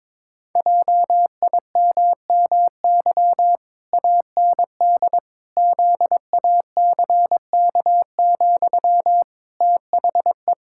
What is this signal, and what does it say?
Morse code, heard as a keyed tone, about 22 wpm, 700 Hz. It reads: JIMMY AND ZACK, THE
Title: Pangram Challenge v2 - 22wpm